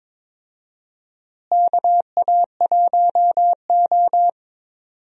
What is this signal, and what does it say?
Morse code, heard as a keyed tone, about 22 wpm, 700 Hz. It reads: KA1O
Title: USA Call Signs - 4 Characters - Repeated 3x - 22wpm